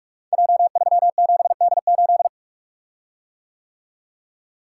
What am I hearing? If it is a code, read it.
Morse code, heard as a keyed tone, about 44 wpm, 700 Hz. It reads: J37D8
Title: Code Groups - 5 Letter:Numbers - 45wpm